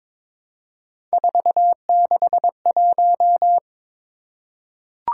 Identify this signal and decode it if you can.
Morse code, heard as a keyed tone, about 22 wpm, 700 Hz. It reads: 461
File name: practice / 22wpm / Three Letter-Number Combinations - 22wpm